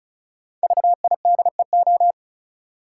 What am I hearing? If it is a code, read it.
Morse code, heard as a keyed tone, about 34 wpm, 700 Hz. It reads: VIDEO
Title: Five Letter Words - 35wpm